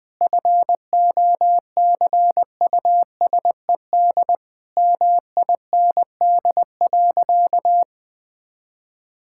Morse code, heard as a keyed tone, about 20 wpm, 700 Hz. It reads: FOCUSED MIND.
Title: Difficult Sentences with Hints - 20wpm